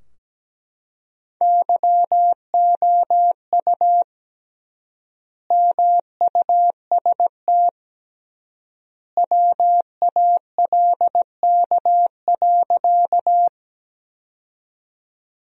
Morse code, heard as a keyed tone, about 17 wpm, 700 Hz. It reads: YOU MUST WALK.
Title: 3x Word-Spacing - Sentences from Top 300 Words - 17wpm